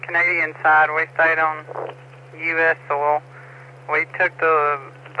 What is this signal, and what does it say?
Emergency Alert System (SAME) tones, heard as harsh buzzing bursts, with speech. It is heard from 0.9 to 4.3 s.